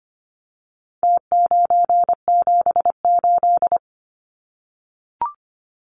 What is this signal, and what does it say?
Morse code, heard as a keyed tone, about 25 wpm, 700 Hz. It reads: T978
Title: Numbers - Contest Style - 4-Digit 25wpm